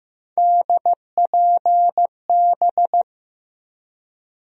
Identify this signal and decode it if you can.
Morse code, heard as a keyed tone, about 15 wpm, 700 Hz. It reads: DPB
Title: USA Call Sign Postfixes - 3 Letters 15wpm